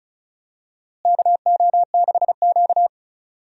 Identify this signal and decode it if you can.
Morse code, heard as a keyed tone, about 36 wpm, 700 Hz. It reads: KO6Q
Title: USA Call Signs - 4 Characters - Repeated 2x - 35wpm